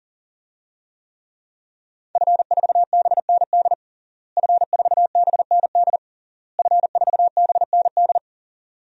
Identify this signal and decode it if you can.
Morse code, heard as a keyed tone, about 40 wpm, 700 Hz. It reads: F4BND F4BND F4BND
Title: DXCC-100 Call Signs - 5 Characters - Repeated 3x - 40wpm